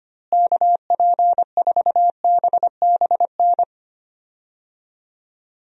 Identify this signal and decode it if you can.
Morse code, heard as a keyed tone, about 25 wpm, 700 Hz. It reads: KP4BBN
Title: DXCC-100 Call Signs - 6 Characters - Repeated 2x - 25wpm